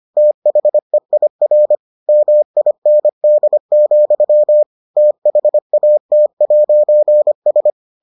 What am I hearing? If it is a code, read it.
Morse code, heard as a keyed tone, about 25 wpm, 600 Hz. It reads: THEIR MIND, THAT'S